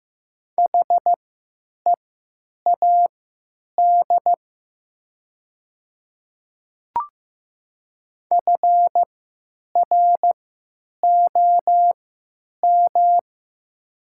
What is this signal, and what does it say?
Morse code, heard as a keyed tone, about 16 wpm, spaced as for 9 wpm, 700 Hz. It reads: HEAD FROM
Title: 3x Character Spacing - Top 300 Words - 15wpm